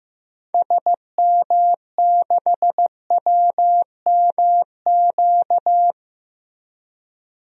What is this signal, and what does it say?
Morse code, heard as a keyed tone, about 15 wpm, 700 Hz. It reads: SM6WMQ
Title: DXCC-100 Call Signs - 6 Characters - Repeated 3x - 15wpm